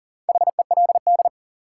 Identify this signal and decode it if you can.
Morse code, heard as a keyed tone, about 40 wpm, 700 Hz. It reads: HELD
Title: Sentences from Top 1600 Words - New Word Drill - 40wpm